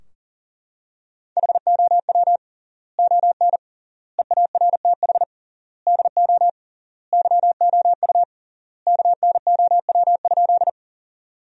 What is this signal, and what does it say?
Morse code, heard as a keyed tone, about 40 wpm, 700 Hz. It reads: HOW ON EARTH DO YOU KNOW?
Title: 3x Word-Spacing - Sentences from Top 200 Words - 40wpm